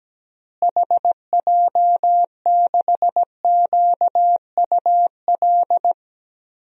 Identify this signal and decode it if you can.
Morse code, heard as a keyed tone, about 17 wpm, 700 Hz. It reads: HJ6QUL
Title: DXCC-100 Call Signs - 6 Characters - Repeated 3x - 17wpm